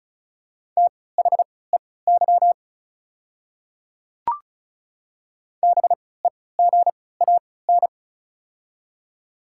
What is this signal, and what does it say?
Morse code, heard as a keyed tone, about 36 wpm, spaced as for 20 wpm, 700 Hz. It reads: THEY BEGAN
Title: 3x Character Spacing - Top 300 Words - 35wpm